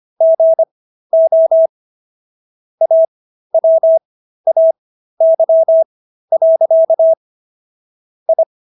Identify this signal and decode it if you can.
Morse code, heard as a keyed tone, about 25 wpm, spaced as for 13 wpm, 650 Hz. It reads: GO AWAY. I